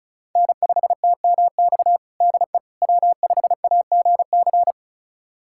Morse code, heard as a keyed tone, about 36 wpm, 700 Hz. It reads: N5TMX DE W5AGC
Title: USA Callsign DE Callsign - 5 Characters - 35wpm